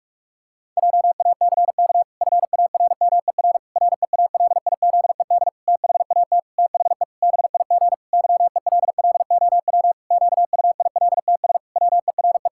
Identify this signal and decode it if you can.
Morse code, heard as a keyed tone, about 41 wpm, 700 Hz. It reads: JACK FARMER REALIZED THAT THE BIG YELLOW QUILTS WERE
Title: Pangram Challenge v2 - 45wpm